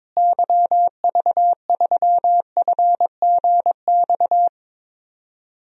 Morse code, heard as a keyed tone, about 22 wpm, 700 Hz. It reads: YV3FGX
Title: DXCC-100 Call Signs - 6 Characters - Repeated 2x - 22wpm